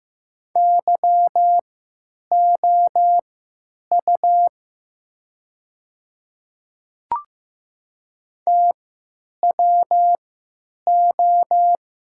Morse code, heard as a keyed tone, about 15 wpm, spaced as for 9 wpm, 700 Hz. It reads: YOU TWO
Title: 3x Character Spacing - Top 100 Words - 15wpm